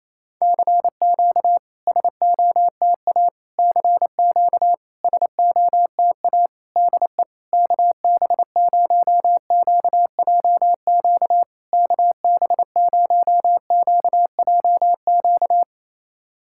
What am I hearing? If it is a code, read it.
Morse code, heard as a keyed tone, about 28 wpm, 700 Hz. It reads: CQ SOTA CQ SOTA DE KB0QJQ KB0QJQ
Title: SOTA QSO Practice 28wpm